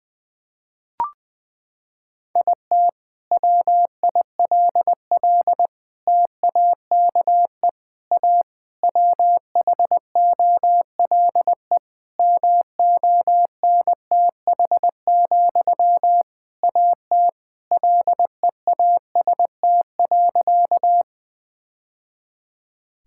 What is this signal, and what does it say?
Morse code, heard as a keyed tone, about 20 wpm, 700 Hz. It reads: IT WILL TAKE A WHOLE MONTH, AT LEAST.
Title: Sentences from Top 1000 Words - Review Part 5 - 20wpm